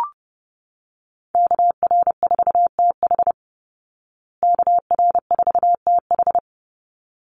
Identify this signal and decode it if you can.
Morse code, heard as a keyed tone, about 30 wpm, 700 Hz. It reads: KR4TH KR4TH
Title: Call Signs Repeated 3x - 30wpm